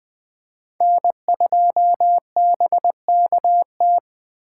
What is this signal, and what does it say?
Morse code, heard as a keyed tone, about 20 wpm, 700 Hz. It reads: N2BKT
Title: USA Call Signs - 8x Character Spacing - 20wpm